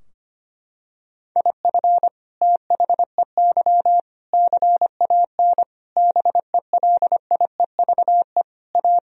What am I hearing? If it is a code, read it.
Morse code, heard as a keyed tone, about 25 wpm, 700 Hz. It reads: IF THEY CAN BELIEVE A
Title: Sentences from Top 1000 Words - Review Part 2 - 25wpm